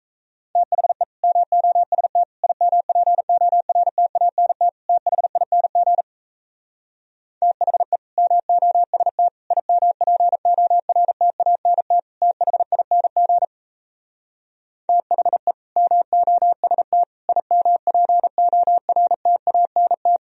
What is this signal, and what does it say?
Morse code, heard as a keyed tone, about 38 wpm, 700 Hz. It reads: THE MOST IMPORTANT THING THE MOST IMPORTANT THING THE MOST IMPORTANT